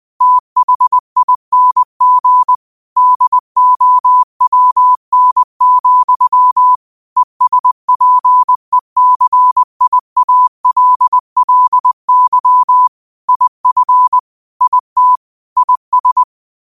Morse code, heard as a keyed tone, about 20 wpm, 1000 Hz. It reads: THING DOWN, ESPECIALLY IF IT IS
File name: qotdfiles / 20wpm